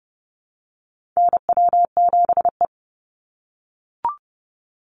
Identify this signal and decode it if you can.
Morse code, heard as a keyed tone, about 30 wpm, 700 Hz. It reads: NW7E
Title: NA SOTA Chaser Callsigns 30wpm